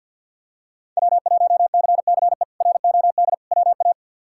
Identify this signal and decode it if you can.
Morse code, heard as a keyed tone, about 52 wpm, 700 Hz. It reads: W1CCE ROD WA
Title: K1USN SST QSO Practice - 50wpm